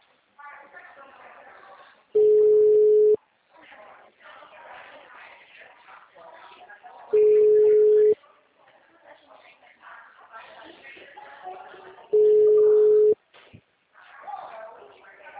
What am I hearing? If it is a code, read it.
Ringback tone (Europe, 425 Hz), with speech.